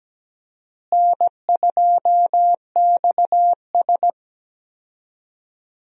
Morse code, heard as a keyed tone, about 17 wpm, 700 Hz. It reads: N2XS
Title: USA Call Signs - 4 Characters - Repeated 2x - 17wpm